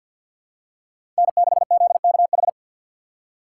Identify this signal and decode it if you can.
Morse code, heard as a keyed tone, about 47 wpm, 700 Hz. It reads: N6ZKH
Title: USA Call Signs - 5 Characters - 4x Character Spacing - 50wpm